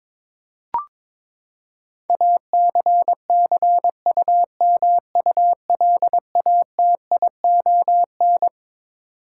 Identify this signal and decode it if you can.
Morse code, heard as a keyed tone, about 22 wpm, 700 Hz. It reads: ACCUMULATION
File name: Twelve Letter Words - 22wpm